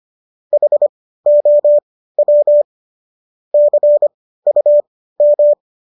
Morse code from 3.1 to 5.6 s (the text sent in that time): CUM